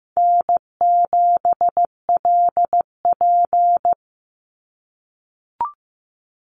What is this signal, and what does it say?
Morse code, heard as a keyed tone, about 15 wpm, 700 Hz. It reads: N7LP
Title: NA SOTA Chaser Callsigns 15wpm